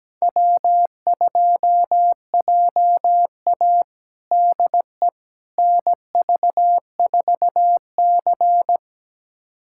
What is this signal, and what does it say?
Morse code, heard as a keyed tone, about 17 wpm, 700 Hz. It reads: W2JA DE NV4C
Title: USA Callsign DE Callsign - 4 Characters - 17wpm